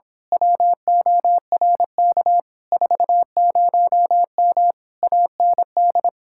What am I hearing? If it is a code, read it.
Morse code, heard as a keyed tone, about 26 wpm, 700 Hz. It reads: WORK 40M AND
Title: Speed-Racing Common QSO Phrases Encore - 22wpm